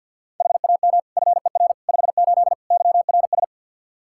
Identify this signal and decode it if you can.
Morse code, heard as a keyed tone, about 49 wpm, 700 Hz. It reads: HAM FER 58 YRS
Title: Practice QSO - Exchange 3 - More Info - 50wpm